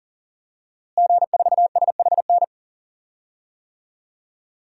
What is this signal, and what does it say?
Morse code, heard as a keyed tone, about 40 wpm, 700 Hz. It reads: G4SHN
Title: DXCC-100 Call Signs - 5 Characters - Repeated 2x - 40wpm